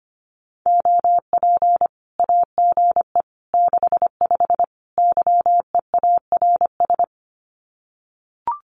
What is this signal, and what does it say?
Morse code, heard as a keyed tone, about 25 wpm, 700 Hz. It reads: OP AGE 65 YEARS
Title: Common QSO Phrases 25wpm